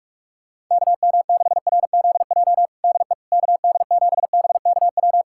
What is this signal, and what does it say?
Morse code, heard as a keyed tone, about 46 wpm, 700 Hz. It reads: KM6RZJ DE KD7BKW
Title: USA Callsign DE Callsign - 6 Characters - 45wpm